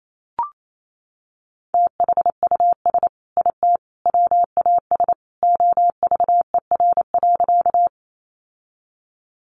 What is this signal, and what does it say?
Morse code, heard as a keyed tone, about 28 wpm, 700 Hz. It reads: THUS IT WAS OVER.
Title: Sentences from Top 800 Words 28wpm